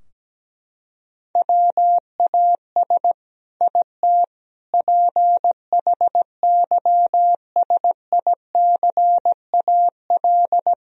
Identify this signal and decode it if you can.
Morse code, heard as a keyed tone, about 17 wpm, 700 Hz. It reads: WAS IT PHYSICAL
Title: Sentences from Top 1000 Words - Review Part 4 - 17wpm